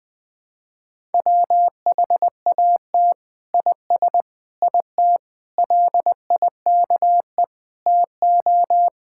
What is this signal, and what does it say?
Morse code, heard as a keyed tone, about 21 wpm, 700 Hz. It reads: WHAT IS IT LIKE TO
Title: Sentences from Top 1000 Words - Review Part 6 - 20wpm